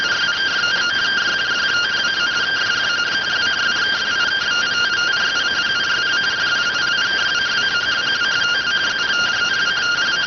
Two-tone teleprinter signal, about 170 Hz shift.